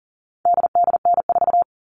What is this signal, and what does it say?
Morse code, heard as a keyed tone, about 40 wpm, 700 Hz. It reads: DDN4
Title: All Cut Numbers - 4-Digit 40wpm